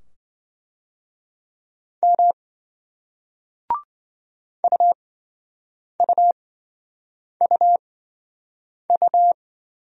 Morse code, heard as a keyed tone, about 27 wpm, spaced as for 7 wpm, 700 Hz. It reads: M UUUU